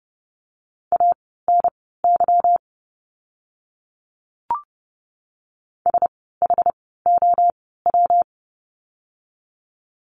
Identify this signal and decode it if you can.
Morse code, heard as a keyed tone, about 30 wpm, spaced as for 17 wpm, 700 Hz. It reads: ANY SHOW
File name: course / 30:10wpm / 230 Lesson 37f - Letter x - Words 30wpm